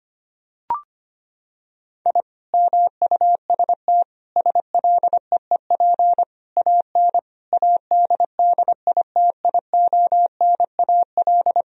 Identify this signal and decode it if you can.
Morse code, heard as a keyed tone, about 25 wpm, 700 Hz. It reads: I MUST SLEEP AN ADDITIONAL
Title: Sentences from Top 1000 Words - Review Part 4 - 25wpm